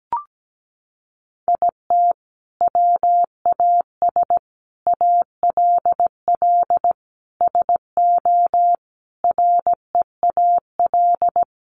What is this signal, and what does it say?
Morse code, heard as a keyed tone, about 17 wpm, 700 Hz. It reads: IT WAS ALL SO REAL
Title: Sentences from Top 1000 Words 17wpm